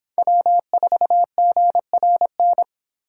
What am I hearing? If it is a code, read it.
Morse code, heard as a keyed tone, about 30 wpm, 700 Hz. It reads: W4GRN